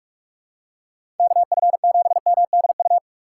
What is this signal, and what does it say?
Morse code, heard as a keyed tone, about 43 wpm, 700 Hz. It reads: KF7MDU
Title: USA Call Signs - 6 Characters - 6x Character Spacing - 45wpm